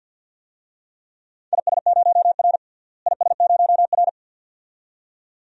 Morse code, heard as a keyed tone, about 51 wpm, 700 Hz. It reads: IS0R IS0R
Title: DXCC-100 Call Signs - 4 Characters - Repeated 2x - 50wpm